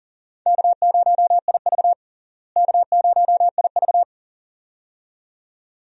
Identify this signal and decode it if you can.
Morse code, heard as a keyed tone, about 37 wpm, 700 Hz. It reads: K0IV K0IV
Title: USA Call Signs - 4 Characters - Repeated 2x - 40wpm